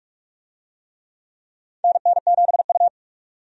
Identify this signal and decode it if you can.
Morse code, heard as a keyed tone, about 45 wpm, 700 Hz. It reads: NN7U
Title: USA Call Signs - 4 Characters - 45wpm